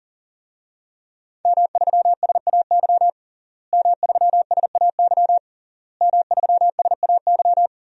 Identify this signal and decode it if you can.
Morse code, heard as a keyed tone, about 40 wpm, 700 Hz. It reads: M3SAY M3SAY M3SAY
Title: DXCC-100 Call Signs - 5 Characters - Repeated 3x - 40wpm